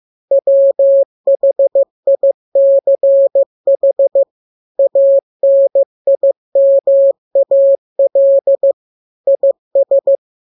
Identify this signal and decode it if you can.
Morse code, heard as a keyed tone, about 15 wpm, 550 Hz. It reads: WHICH ANIMAL IS